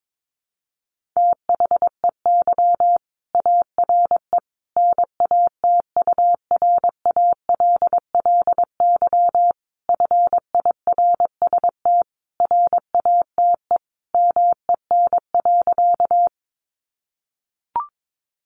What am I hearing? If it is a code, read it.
Morse code, heard as a keyed tone, about 22 wpm, 700 Hz. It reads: THEY ARE NATURALLY FIRST RATE MEN.
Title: Sentences from Top 1300 Words 22wpm